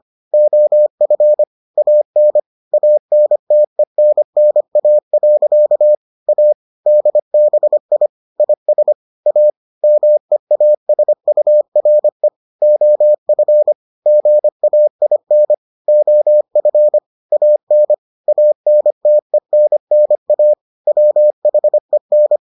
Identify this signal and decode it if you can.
Morse code, heard as a keyed tone, about 25 wpm, 600 Hz. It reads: OF AN ANTENNA. A DBI IS A MEASURE OF GAIN OF AN ANTENNA WHEN